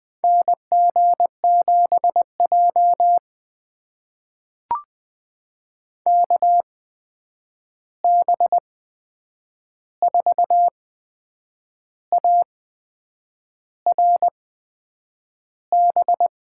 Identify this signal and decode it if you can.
Morse code, heard as a keyed tone, about 20 wpm, spaced as for 8 wpm, 700 Hz. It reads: NG7J KB4ARB
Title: USA Call Signs - 8x Character Spacing - 20wpm